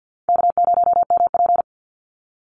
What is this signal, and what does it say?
Morse code, heard as a keyed tone, about 50 wpm, 700 Hz. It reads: K0MP
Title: Easy Call Signs - 50wpm